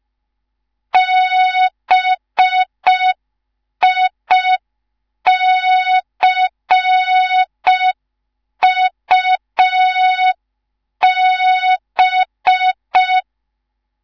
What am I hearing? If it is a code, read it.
Morse code, heard as a keyed tone, about 5 wpm, 3700 Hz. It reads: BICUB
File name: Morze-HA2MN / hang / 19 GYAKORLAT